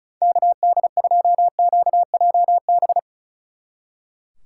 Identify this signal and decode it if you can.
Morse code, heard as a keyed tone, about 44 wpm, 700 Hz. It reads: KD2QJB